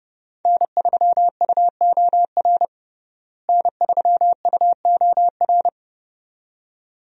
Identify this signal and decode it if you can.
Morse code, heard as a keyed tone, about 30 wpm, 700 Hz. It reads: N3UOR N3UOR
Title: USA Call Signs - 5 Characters - Repeated 2x - 30wpm